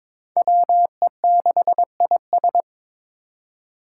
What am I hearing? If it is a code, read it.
Morse code, heard as a keyed tone, about 22 wpm, 700 Hz. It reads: WE6IS
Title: Five Challenging Letter-Number Combinations 22wpm